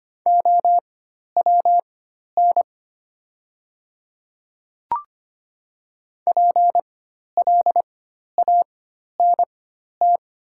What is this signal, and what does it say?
Morse code, heard as a keyed tone, about 25 wpm, spaced as for 12 wpm, 690 Hz. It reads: OWN PLANT
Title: 4x Character Spacing - Top 500 Words - 25wpm